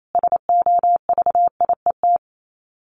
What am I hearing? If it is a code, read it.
Morse code, heard as a keyed tone, about 28 wpm, 700 Hz. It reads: SOVIET
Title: Sets of 5 Words Encore 28wpm